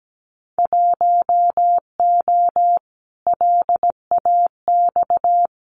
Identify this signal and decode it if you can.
Morse code, heard as a keyed tone, about 17 wpm, 700 Hz. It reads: 1O LAX
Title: Winter Field Day Exchanges 17wpm